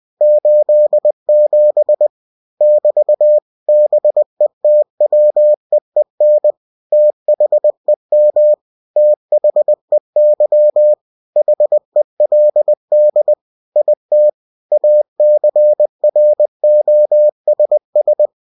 Morse code, heard as a keyed tone, about 20 wpm, 600 Hz. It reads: 87 <BT> BETWEEN THEM THEY HELD IT ACROSS
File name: cw-books / winnie-the-pooh / winnie-20wpm-page-0087